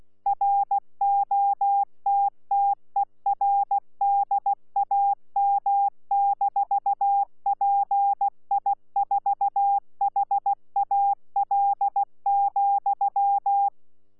Morse code, heard as a keyed tone, about 16 wpm, 800 Hz. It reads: ROTTERDAM-PI4HAL,
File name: Z PI4VRZ / PI4VRZ 14-jun-2025